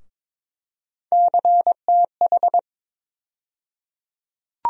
Morse code, heard as a keyed tone, about 22 wpm, 700 Hz. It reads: CTH
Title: USA Call Sign Postfixes - 3 Letters 22wpm